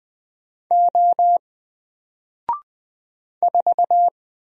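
Morse code, heard as a keyed tone, about 20 wpm, 700 Hz. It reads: O 4
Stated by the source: Single letter and number Morse drill at 20 wpm, from N4LCD.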